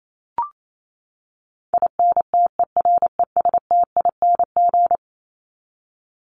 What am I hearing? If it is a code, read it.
Morse code, heard as a keyed tone, about 28 wpm, 700 Hz. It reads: INTERESTING
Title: ING Suffix - Top 100 Words with Base Words 28wpm